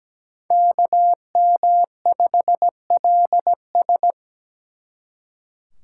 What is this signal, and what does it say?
Morse code, heard as a keyed tone, about 20 wpm, 700 Hz. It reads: KM5LS
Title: Speed Racing - USA 2x2 Call Signs 17wpm